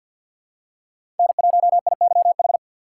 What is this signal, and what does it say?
Morse code, heard as a keyed tone, about 53 wpm, 700 Hz. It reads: N1IYH
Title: USA Call Signs - 4x Character Spacing - 50wpm